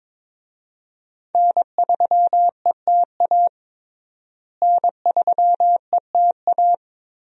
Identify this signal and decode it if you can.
Morse code, heard as a keyed tone, about 22 wpm, 700 Hz. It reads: N3ETA N3ETA
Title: USA Call Signs - 5 Characters - Repeated 2x - 22wpm